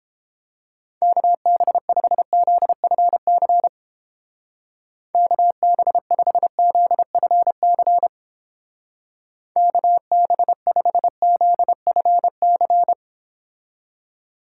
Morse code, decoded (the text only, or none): KB5ZFC KB5ZFC KB5ZFC